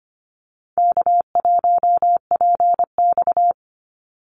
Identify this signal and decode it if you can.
Morse code, heard as a keyed tone, about 25 wpm, 700 Hz. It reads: K1PX
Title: Call Signs - 25wpm